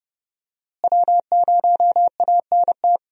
Morse code, heard as a keyed tone, about 29 wpm, 700 Hz. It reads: W0ANT
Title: USA Call Signs - 5 Characters - Repeated 2x - 30wpm